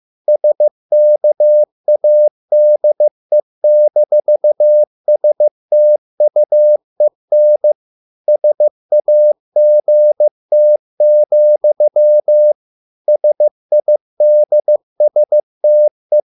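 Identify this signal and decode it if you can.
Morse code, heard as a keyed tone, about 15 wpm, 600 Hz. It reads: SKADE-STUEN SAGT, SIDSTE
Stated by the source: Morse code at a slow 15 wpm from CW med Gnister.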